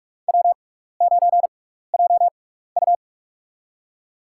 Morse code, heard as a keyed tone, about 42 wpm, spaced as for 15 wpm, 700 Hz. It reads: W9JU
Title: USA Call Signs - 6x Character Spacing - 45wpm